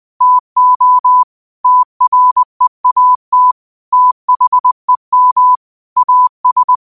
Morse code, heard as a keyed tone, about 20 wpm, 1000 Hz. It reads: TO TREAT THEM AS
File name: qotdfiles / 20wpm